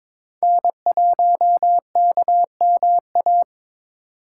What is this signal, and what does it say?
Morse code, heard as a keyed tone, about 27 wpm, 700 Hz. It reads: N1KMA
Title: Speed Racing - USA Call Signs 22wpm